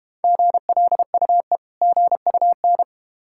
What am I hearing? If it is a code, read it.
Morse code, heard as a keyed tone, about 32 wpm, 700 Hz. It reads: GLUE GUN